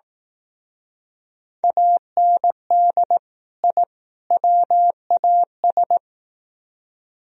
Morse code, heard as a keyed tone, about 18 wpm, 700 Hz. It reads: AND I WAS
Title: Speed-Racing - Sets of 3 Words 15wpm